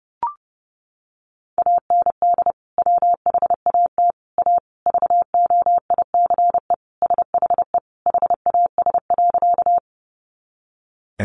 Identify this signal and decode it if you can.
Morse code, heard as a keyed tone, about 30 wpm, 700 Hz. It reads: AND WHAT A VOICE SHE HAS.
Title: Sentences from Top 500 Words 30wpm